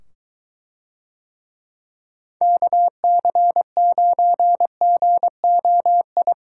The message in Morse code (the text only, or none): KC9GOI